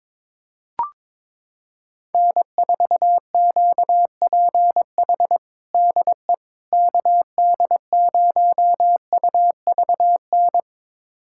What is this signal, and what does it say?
Morse code, heard as a keyed tone, about 21 wpm, 700 Hz. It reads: N4QPH DE KD0UVN
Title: USA Callsign DE Callsign 22wpm